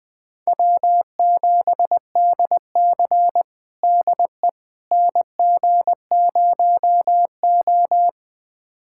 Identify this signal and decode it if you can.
Morse code, heard as a keyed tone, about 20 wpm, 700 Hz. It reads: W7DC DE NG0O
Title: USA Callsign DE Callsign - 4 Characters - 20wpm